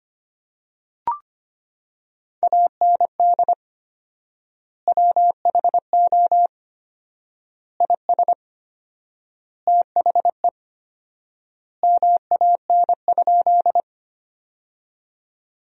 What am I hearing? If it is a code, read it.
Morse code, heard as a keyed tone, about 25 wpm, 700 Hz. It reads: AND WHO IS THE MAN?
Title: 4x Word-Spacing - Sentences from Top 200 Words - 25wpm